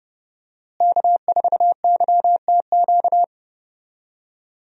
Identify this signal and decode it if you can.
Morse code, heard as a keyed tone, about 30 wpm, 700 Hz. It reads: K4YTQ
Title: USA Call Signs - 5 Characters - Repeated 3x - 30wpm